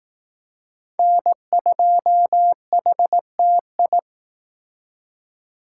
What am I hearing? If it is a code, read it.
Morse code, heard as a keyed tone, about 18 wpm, 700 Hz. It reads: N2HTI
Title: Speed Racing - USA Call Signs 15wpm